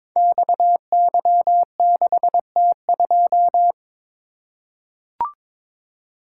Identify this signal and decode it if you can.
Morse code, heard as a keyed tone, about 22 wpm, 700 Hz. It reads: XY6T2
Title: Code Groups - 5 Letter:Numbers - 22wpm